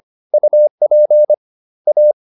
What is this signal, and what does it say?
Morse code, heard as a keyed tone, about 25 wpm, 600 Hz. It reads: UP A
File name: the old man and the sea part 1 25wpm